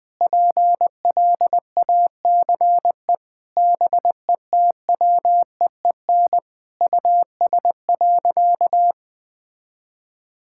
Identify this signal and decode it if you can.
Morse code, heard as a keyed tone, about 21 wpm, 700 Hz. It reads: PLACE BETWEEN US.
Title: Sentences from Top 1000 Words - Review Part 6 - 20wpm